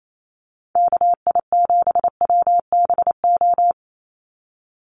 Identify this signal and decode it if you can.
Morse code, heard as a keyed tone, about 28 wpm, 700 Hz. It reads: KI7WBO
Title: USA Call Signs 28wpm